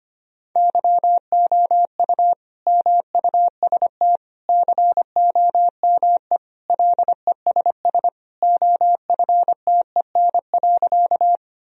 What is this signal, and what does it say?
Morse code, heard as a keyed tone, about 25 wpm, 700 Hz. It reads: YOU MUST COME LESS OFTEN.
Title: Sentences from Top 1000 Words - Review Part 2 - 25wpm